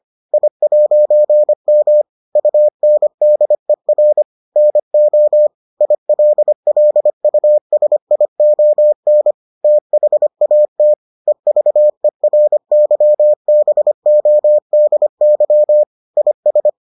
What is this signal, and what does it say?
Morse code, heard as a keyed tone, about 25 wpm, 600 Hz. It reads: I'M UNDER NO ILLUSION THAT EVERYBODY IS